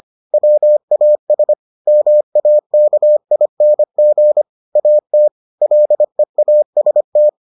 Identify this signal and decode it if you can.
Morse code, heard as a keyed tone, about 25 wpm, 600 Hz. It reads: WAS MAKING AT LEAST